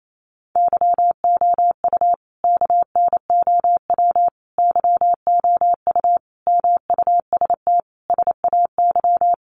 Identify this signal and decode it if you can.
Morse code, heard as a keyed tone, about 28 wpm, 700 Hz. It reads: YOU KNOW YOU MUST SAY
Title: Sentences from Top 500 Words 28wpm